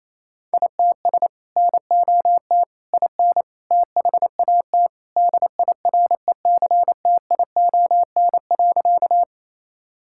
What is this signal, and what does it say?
Morse code, heard as a keyed tone, about 28 wpm, 700 Hz. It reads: ITS NOT IN THAT DIRECTION.
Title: Sentences from Top 1000 Words - Review Part 5 - 28wpm